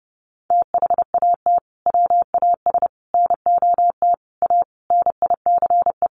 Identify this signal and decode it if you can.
Morse code, heard as a keyed tone, about 30 wpm, 700 Hz. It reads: THAT WAS NOT A NICE
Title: Sentences from Top 1200 Words 30wpm